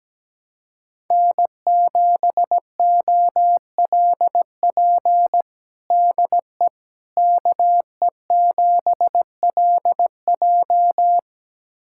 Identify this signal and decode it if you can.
Morse code, heard as a keyed tone, about 17 wpm, 700 Hz. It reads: N7OLP DE KE7LJ
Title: USA Callsign DE Callsign - 5 Characters - 17wpm